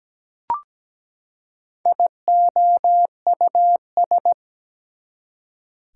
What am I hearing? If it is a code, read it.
Morse code, heard as a keyed tone, about 17 wpm, 700 Hz. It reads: IOUS
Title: Prefixes and Suffixes 17wpm